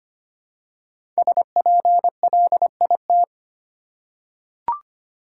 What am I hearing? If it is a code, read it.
Morse code, heard as a keyed tone, about 25 wpm, 700 Hz. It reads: SPLIT
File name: 5 Letter Words - First 3 Letters Given - 25wpm